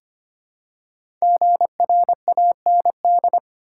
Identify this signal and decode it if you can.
Morse code, heard as a keyed tone, about 25 wpm, 700 Hz. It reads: GRAND
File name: 5 Letter Words - First 3 Letters Given - 25wpm